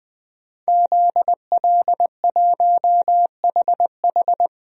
Morse code, heard as a keyed tone, about 21 wpm, 700 Hz. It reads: ZL1HH
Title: DXCC-100 Call Signs - 5 Characters - 20wpm